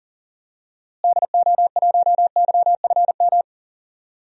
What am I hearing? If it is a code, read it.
Morse code, heard as a keyed tone, about 38 wpm, 700 Hz. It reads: DO1YFM
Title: DXCC-100 Call Signs - 6 Characters - Repeated 2x - 40wpm